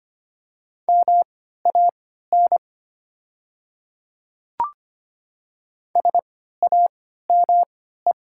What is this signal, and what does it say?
Morse code, heard as a keyed tone, about 25 wpm, spaced as for 14 wpm, 690 Hz. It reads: MAN SAME
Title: 3x Character Spacing - Top 200 Words - 25wpm